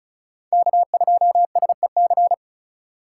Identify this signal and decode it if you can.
Morse code, heard as a keyed tone, about 41 wpm, 700 Hz. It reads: K2SEC
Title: Speed Racing - USA Call Signs 35wpm